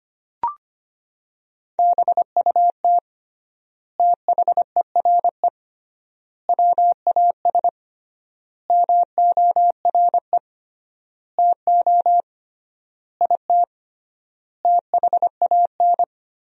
Morse code, heard as a keyed tone, about 25 wpm, 700 Hz. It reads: BUT THERE WAS MORE TO IT THAN
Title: 3x Word-Spacing - Sentences from Top 100 Words - 25wpm